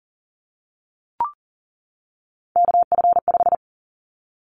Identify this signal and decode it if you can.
Morse code, heard as a keyed tone, about 40 wpm, 700 Hz. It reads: KF5
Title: Callsign Prefixes for Top 25 DXCC - 40wpm